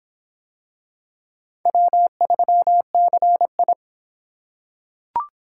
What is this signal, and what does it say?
Morse code, heard as a keyed tone, about 26 wpm, 700 Hz. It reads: W3CI